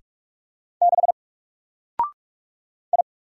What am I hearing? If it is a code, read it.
Morse code, heard as a keyed tone, about 46 wpm, 700 Hz. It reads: 6 I
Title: Single Character (Letter:Number:Punctuation:Prosign) 45wpm